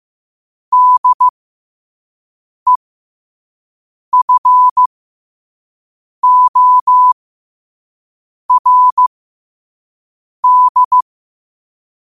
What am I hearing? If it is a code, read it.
Morse code, heard as a keyed tone, about 15 wpm, spaced as for 5 wpm, 1000 Hz. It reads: DEFORD
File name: qotdfiles / 5wpm